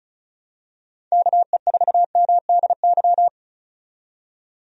Morse code, heard as a keyed tone, about 36 wpm, 700 Hz. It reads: KE4MDY
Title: USA Call Signs - 6 Characters - 35wpm